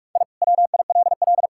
Morse code, heard as a keyed tone, about 46 wpm, 700 Hz. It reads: I WILL
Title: Sentences from Top 1000 Words - Review Part 6 - 45wpm